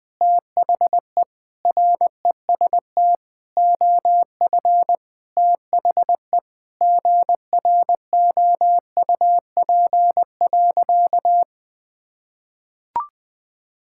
Morse code, heard as a keyed tone, about 20 wpm, 700 Hz. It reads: THE REST OF THE GROUP.
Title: Sentences with 5-Word Phrases - Spoken Beginning - 20wpm